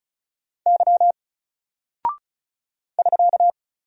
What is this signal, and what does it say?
Morse code, heard as a keyed tone, about 36 wpm, 700 Hz. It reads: Y <SK>
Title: Single Character (Letter:Number:Punctuation:Prosign) 35wpm